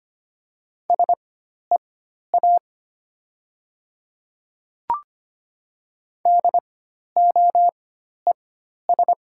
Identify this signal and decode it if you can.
Morse code, heard as a keyed tone, about 25 wpm, spaced as for 12 wpm, 690 Hz. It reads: SEA DOES
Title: 4x Character Spacing - Top 300 Words - 25wpm